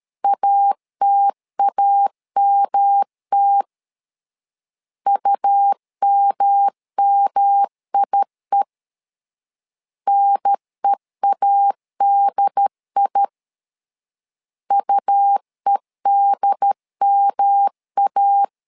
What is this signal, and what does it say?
Morse code, heard as a keyed tone, about 13 wpm, 800 Hz. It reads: ATAMT UMMIE NEADI UEDMA
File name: LEZIONI / 3 MWRO / 3 M 12wpm 1541